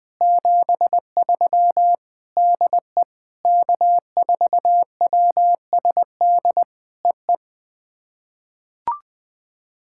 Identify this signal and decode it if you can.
Morse code, heard as a keyed tone, about 20 wpm, 700 Hz. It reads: 73 DE K4WSD EE
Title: POTA QSO Practice - 20wpm